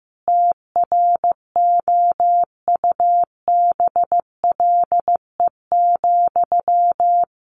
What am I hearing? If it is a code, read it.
Morse code, heard as a keyed tone, about 15 wpm, 700 Hz. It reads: TROUBLE,
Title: Sentences from Top 1200 Words 15wpm